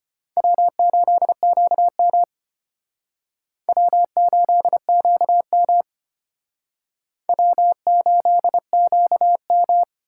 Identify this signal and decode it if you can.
Morse code, heard as a keyed tone, about 31 wpm, 700 Hz. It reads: W8QM W8QM W8QM
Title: Speed Racing - USA 1x2 Call Signs 25wpm